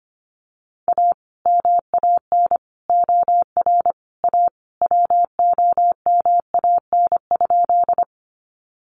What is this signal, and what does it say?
Morse code, heard as a keyed tone, about 25 wpm, 700 Hz. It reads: A MAN OR A WOMAN?
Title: Sentences from Top 1100 Words 25wpm